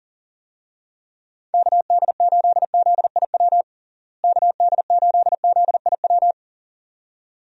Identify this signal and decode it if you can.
Morse code, heard as a keyed tone, about 40 wpm, 700 Hz. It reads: KD8ZIW KD8ZIW
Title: USA Call Signs - 6 Characters - Repeated 2x - 40wpm